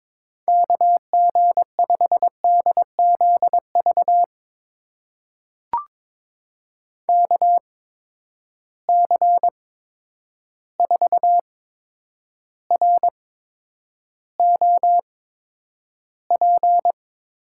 Morse code, heard as a keyed tone, about 22 wpm, spaced as for 9 wpm, 700 Hz. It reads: KG5DZV KC4ROP
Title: USA Call Signs - 6 Characters - 8x Character Spacing - 22wpm